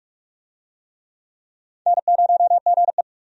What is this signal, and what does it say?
Morse code, heard as a keyed tone, about 46 wpm, 700 Hz. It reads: N0GE
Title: USA Call Signs - 4 Characters - 45wpm